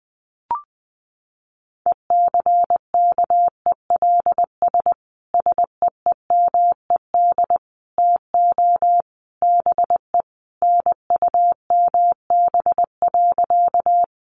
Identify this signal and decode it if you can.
Morse code, heard as a keyed tone, about 19 wpm, 700 Hz. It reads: ECKELS SEEMED TO BE NUMB.
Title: A Sound Of Thunder 20wpm